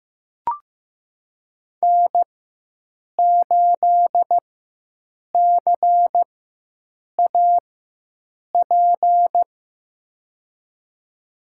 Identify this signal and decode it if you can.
Morse code, heard as a keyed tone, about 15 wpm, spaced as for 7 wpm, 700 Hz. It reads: N8CAP
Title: USA Call Signs - 5 Characters - 4x Character Spacing - 15wpm